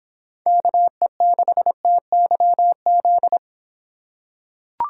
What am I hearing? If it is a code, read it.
Morse code, heard as a keyed tone, about 26 wpm, 700 Hz. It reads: KE6TYZ